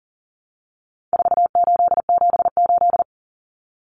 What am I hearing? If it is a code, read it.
Morse code, heard as a keyed tone, about 40 wpm, 700 Hz. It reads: 4878
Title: Common Cut Numbers - 4-Digit 40wpm